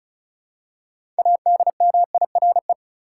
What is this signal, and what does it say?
Morse code, heard as a keyed tone, about 36 wpm, 700 Hz. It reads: ADMIRE
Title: Six Letter Words - 35wpm